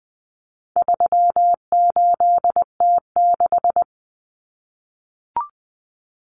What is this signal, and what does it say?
Morse code, heard as a keyed tone, about 20 wpm, 700 Hz. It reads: 38T6
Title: Common Cut Numbers - 4-Digit 20wpm